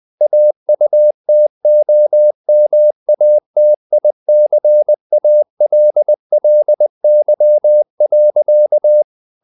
Morse code, heard as a keyed tone, about 20 wpm, 600 Hz. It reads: AUTOMATICALLY.
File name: the old man and the sea / the old man and the sea part 1 20wpm